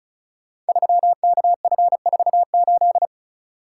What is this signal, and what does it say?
Morse code, heard as a keyed tone, about 36 wpm, 700 Hz. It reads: 3KF48
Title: Code Groups - 5 Letter:Numbers - 35wpm